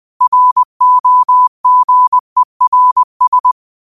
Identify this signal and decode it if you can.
Morse code, heard as a keyed tone, about 20 wpm, 1000 Hz. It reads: ROGERS
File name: qotdfiles / 20wpm